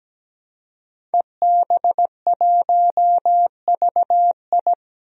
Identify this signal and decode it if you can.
Morse code, heard as a keyed tone, about 17 wpm, 700 Hz. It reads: EB1VI
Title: DXCC-100 Call Signs - 5 Characters - 17wpm